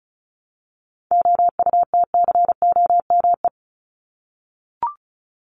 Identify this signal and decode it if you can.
Morse code, heard as a keyed tone, about 34 wpm, 700 Hz. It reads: OUTCOME
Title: Top 1501-1750 Words 35wpm